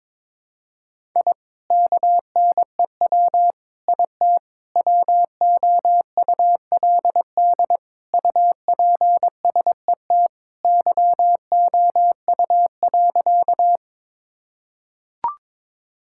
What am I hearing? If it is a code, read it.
Morse code, heard as a keyed tone, about 22 wpm, 700 Hz. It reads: I KNEW IT WOULD UPSET YOU.
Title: Sentences from Top 1000 Words - Review Part 5 - 22wpm